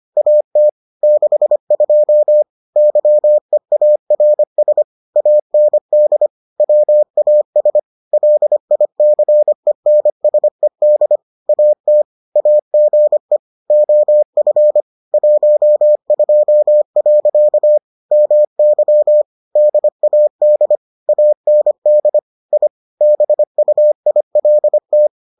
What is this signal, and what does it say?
Morse code, heard as a keyed tone, about 25 wpm, 600 Hz. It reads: AT 62 YEARS AND WAS LICENSED AT AGE OF 12. MY DAD AND I BUILT